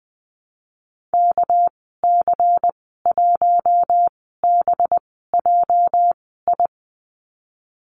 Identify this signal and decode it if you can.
Morse code, heard as a keyed tone, about 20 wpm, spaced as for 15 wpm, 700 Hz. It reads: KC1BJI
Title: 231 Lesson 37g - Letter x - Call Signs 20wpm